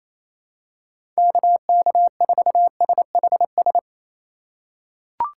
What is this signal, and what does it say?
Morse code, heard as a keyed tone, about 28 wpm, 700 Hz. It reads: KK4SHS
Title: USA Call Signs - 6 Characters - 3x Character Spacing - 28wpm